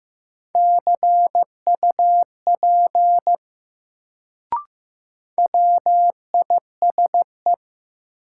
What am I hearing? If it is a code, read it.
Morse code, heard as a keyed tone, about 15 wpm, 700 Hz. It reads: CUP WISE
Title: Sentences from Top 1600 Words - New Word Drill Alt - 15wpm